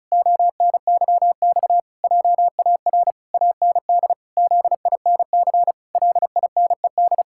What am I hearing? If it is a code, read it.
Morse code, heard as a keyed tone, about 36 wpm, 700 Hz. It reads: ONYX JAR AND ZINC LINED
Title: Pangram Challenge v2 - 35wpm